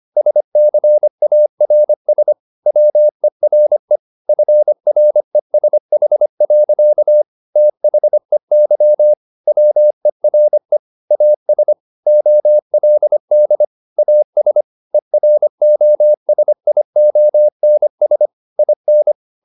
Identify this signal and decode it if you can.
Morse code, heard as a keyed tone, about 25 wpm, 600 Hz. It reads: SCARS WERE FRESH. THEY WERE AS OLD AS EROSIONS IN
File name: the old man and the sea part 1 25wpm